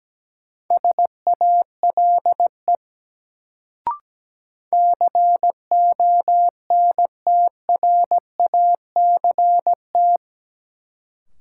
Morse code, heard as a keyed tone, about 17 wpm, 700 Hz. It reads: SALE CONTRACT
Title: Sentences from Top 1400 Words - New Word Drill - 17wpm